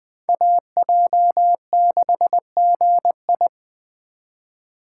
Morse code, heard as a keyed tone, about 21 wpm, 700 Hz. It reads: AJ6GI
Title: USA Call Signs - 5 Characters - 4x Character Spacing - 20wpm